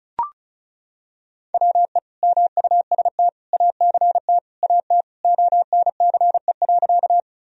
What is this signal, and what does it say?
Morse code, heard as a keyed tone, about 36 wpm, 700 Hz. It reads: WE MUST ACT AT ONCE.
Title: Sentences from Top 1000 Words - Review Part 1 - 35wpm